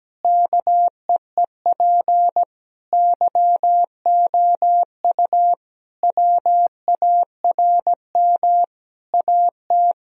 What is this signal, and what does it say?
Morse code, heard as a keyed tone, about 17 wpm, 700 Hz. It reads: KEEP YOU WARM AT
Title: Sentences from Top 1000 Words - Review Part 2 - 17wpm